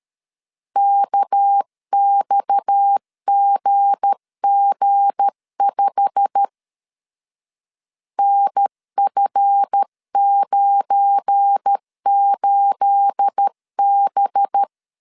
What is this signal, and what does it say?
Morse code, heard as a keyed tone, about 13 wpm, 800 Hz. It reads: KXGG5 NF98B
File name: LEZIONI / 9 28 12wpm 2351